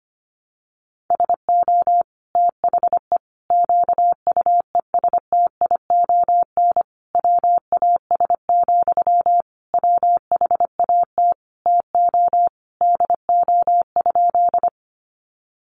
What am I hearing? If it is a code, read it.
Morse code, heard as a keyed tone, about 25 wpm, 700 Hz. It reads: SO THE QUESTION WAS, WHAT TO DO?
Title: Sentences from Top 400 Words 25wpm